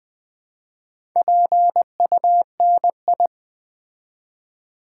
Morse code, heard as a keyed tone, about 20 wpm, 700 Hz. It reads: PUNI
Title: Build A Word - 2 Letters At A Time - 6-Letter Words - 20wpm